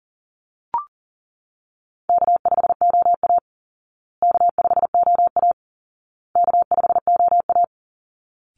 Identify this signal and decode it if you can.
Morse code, heard as a keyed tone, about 37 wpm, 700 Hz. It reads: K5OA K5OA K5OA
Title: Easy Call Signs Repeated 3x - 40wpm